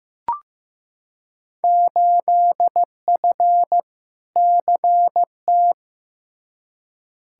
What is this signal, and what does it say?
Morse code, heard as a keyed tone, about 15 wpm, 700 Hz. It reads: 8F CT
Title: Field Day Exchanges Encore - 15wpm